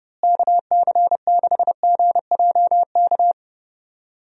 Morse code, heard as a keyed tone, about 32 wpm, 700 Hz. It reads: KC6GJK
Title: USA Call Signs - 6 Characters - Repeated 2x - 30wpm